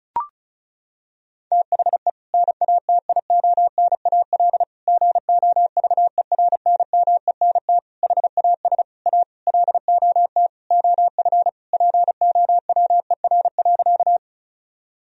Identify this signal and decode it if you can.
Morse code, heard as a keyed tone, about 34 wpm, 700 Hz. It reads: THE NATIONAL GOVERNMENT HAS A LOT OF POWER.
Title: Sentences from Top 1000 Words - Review Part 4 - 35wpm